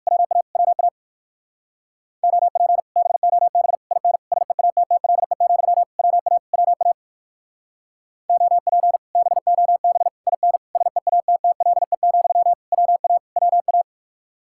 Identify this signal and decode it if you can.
Morse code, heard as a keyed tone, about 59 wpm, 700 Hz. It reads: WA WA OP BOB IN SEATTLE, WA WA OP BOB IN SEATTLE, WA WA